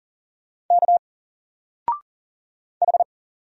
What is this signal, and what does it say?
Morse code, heard as a keyed tone, about 40 wpm, 700 Hz. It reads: K H
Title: Single Letter-Number 40wpm